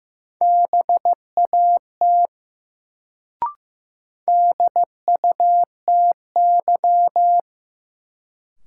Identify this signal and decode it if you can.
Morse code, heard as a keyed tone, about 15 wpm, 700 Hz. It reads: BAT DUTY
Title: Sentences from Top 1500 Words - New Word Drill - 15wpm